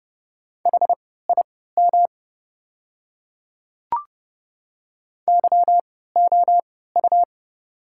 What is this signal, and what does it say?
Morse code, heard as a keyed tone, about 32 wpm, spaced as for 17 wpm, 700 Hz. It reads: HIM YOU
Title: 3x Character Spacing - Top 200 Words - 30wpm